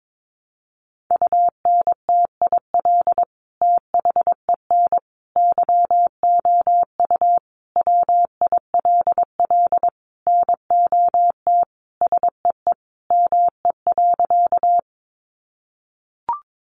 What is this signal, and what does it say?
Morse code, heard as a keyed tone, about 22 wpm, 700 Hz. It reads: UNTIL THEN YOU WILL NOT SEE ME.
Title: Sentences from Top 600 Words 22wpm